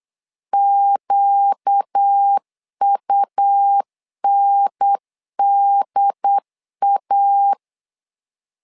Morse code, heard as a keyed tone, about 8 wpm, 800 Hz. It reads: QUNDA
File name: LEZIONI / 4 Q 8wpm 2732